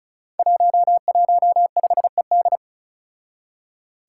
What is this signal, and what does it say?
Morse code, heard as a keyed tone, about 36 wpm, 700 Hz. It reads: 115ED
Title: Five Challenging Letter-Number Combinations 35wpm